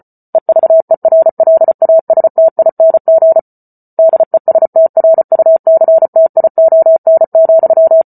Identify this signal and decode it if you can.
Morse code, heard as a keyed tone, about 34 wpm, 650 Hz. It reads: EVERLASTING DESTRUCTION,